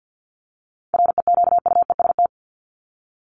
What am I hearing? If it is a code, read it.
Morse code, heard as a keyed tone, about 48 wpm, 700 Hz. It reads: REQUEST
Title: Top 1751-2000 Words 50wpm